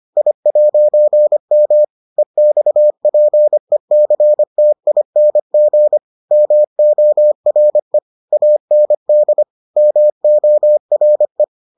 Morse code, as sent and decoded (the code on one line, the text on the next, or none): .. .----. -- / . -..- .--. -.-. - .. -. --. / -- --- .-. / .- -. -.. / -- --- .-. .
I'M EXPECTING MORE AND MORE